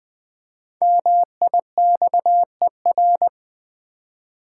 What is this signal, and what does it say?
Morse code, heard as a keyed tone, about 20 wpm, 700 Hz. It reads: MIXER
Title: Easy Words with Hints - 20wpm